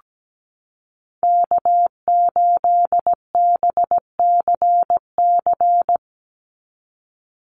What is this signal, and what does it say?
Morse code, heard as a keyed tone, about 17 wpm, 700 Hz. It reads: K8BCC
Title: USA Call Signs 17wpm